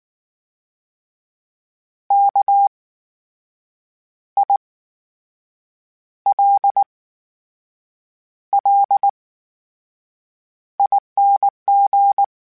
Morse code, heard as a keyed tone, about 19 wpm, spaced as for 6 wpm, 800 Hz. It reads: KILLING
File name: Special INGs / Special INGs-mp3s / VerySlowToFaster Ing @18